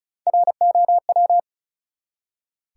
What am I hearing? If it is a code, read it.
Morse code, heard as a keyed tone, about 36 wpm, 700 Hz. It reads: ROW
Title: Three Letter Words - 35wpm